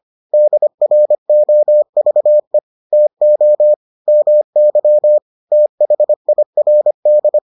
Morse code, heard as a keyed tone, about 25 wpm, 600 Hz. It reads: DROVE TO MY THIRD